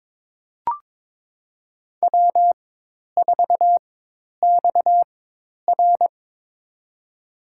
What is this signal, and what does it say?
Morse code, heard as a keyed tone, about 22 wpm, spaced as for 10 wpm, 700 Hz. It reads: W4XR
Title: USA Call Signs - 4x Character Spacing - 22wpm